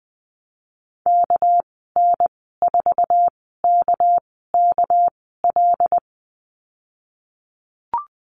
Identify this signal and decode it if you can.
Morse code, heard as a keyed tone, about 20 wpm, spaced as for 14 wpm, 700 Hz. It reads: KN4KKL
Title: 238 Lesson 38g - Letter q - Call Signs 20wpm